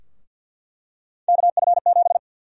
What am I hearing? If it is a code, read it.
Morse code, heard as a keyed tone, about 53 wpm, 700 Hz. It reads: KF7
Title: USA Call Sign Prefixes - 50wpm